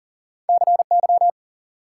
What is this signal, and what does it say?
Morse code, heard as a keyed tone, about 53 wpm, spaced as for 46 wpm, 700 Hz. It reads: CY
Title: Speed-Racing Multi-letter Phonemes - 40wpm